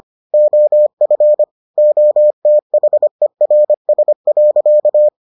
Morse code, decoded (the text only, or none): OF OTHERS.